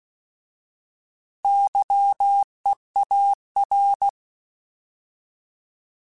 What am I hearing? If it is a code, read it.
Morse code, heard as a keyed tone, about 16 wpm, 800 Hz. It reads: YEAR
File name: Alphabetical / Alphabetical mp3s / Ys @15